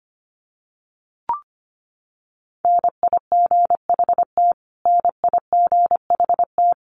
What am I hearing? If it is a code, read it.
Morse code, heard as a keyed tone, about 25 wpm, 700 Hz. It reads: NIGHT NIGHT
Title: Top 300 Words Repeated 25wpm